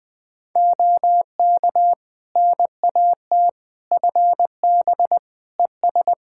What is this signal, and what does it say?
Morse code, heard as a keyed tone, about 20 wpm, 700 Hz. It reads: OK NAT FB ES
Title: Practice QSO - Exchange 2 - Info - 20wpm